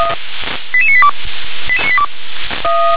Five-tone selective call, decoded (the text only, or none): 88982